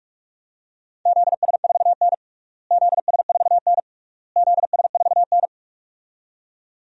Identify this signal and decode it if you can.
Morse code, heard as a keyed tone, about 42 wpm, 700 Hz. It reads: ZS4N ZS4N ZS4N
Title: DXCC-100 Call Signs - 4 Characters - Repeated 3x - 45wpm